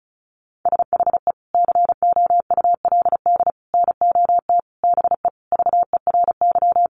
Morse code, heard as a keyed tone, about 34 wpm, 700 Hz. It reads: SHE COULD NOT BE VERY
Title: Sentences from Top 1400 Words 35wpm